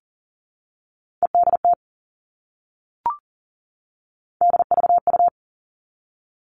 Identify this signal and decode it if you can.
Morse code, heard as a keyed tone, about 40 wpm, 700 Hz. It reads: EDT DVU
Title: All Cut Numbers - 3-Digit 40wpm